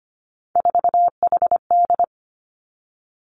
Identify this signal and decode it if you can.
Morse code, heard as a keyed tone, about 25 wpm, 700 Hz. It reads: 4HD
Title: Three Challenging Letter-Number Combinations 25wpm